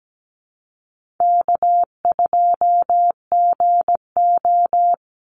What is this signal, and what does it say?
Morse code, heard as a keyed tone, about 17 wpm, 700 Hz. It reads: K2GO
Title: Call Signs Repeated 2x - 17wpm